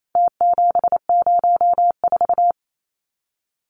Morse code, heard as a keyed tone, about 28 wpm, 700 Hz. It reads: T704
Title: Numbers - Contest Style - 4-Digit 28wpm